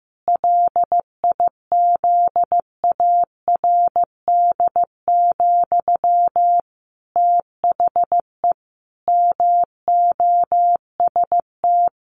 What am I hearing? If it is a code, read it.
Morse code, heard as a keyed tone, about 15 wpm, 700 Hz. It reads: LIZARD, THE MOST
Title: A Sound Of Thunder 15wpm